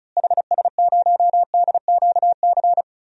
Morse code, heard as a keyed tone, about 34 wpm, 700 Hz. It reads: HS0DQC
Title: DXCC-100 Call Signs - 6 Characters - Repeated 2x - 35wpm